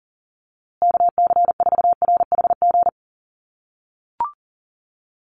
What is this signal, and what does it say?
Morse code, heard as a keyed tone, about 40 wpm, 700 Hz. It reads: KC4RHG
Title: USA Call Signs Repeated 2x 40wpm